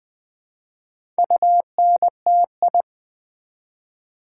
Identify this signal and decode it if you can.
Morse code, heard as a keyed tone, about 20 wpm, 700 Hz. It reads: UNTI
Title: Build A Word - 1 Letter At A Time - 5-Letter Words - 20wpm